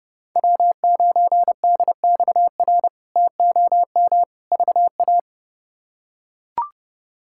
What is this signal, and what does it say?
Morse code, heard as a keyed tone, about 32 wpm, 700 Hz. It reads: W9DXR TOM VA
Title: K1USN SST QSO Practice - 30wpm